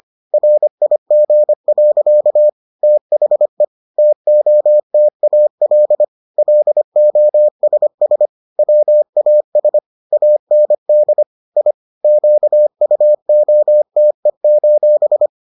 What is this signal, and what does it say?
Morse code, heard as a keyed tone, about 25 wpm, 600 Hz. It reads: RIG. THE TOTAL LOSS WAS AND I QUOTE: